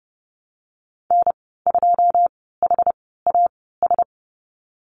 Morse code, heard as a keyed tone, about 30 wpm, spaced as for 17 wpm, 700 Hz. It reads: N2HAS
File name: Lesson 14g - Number 2 - Call Signs 30wpm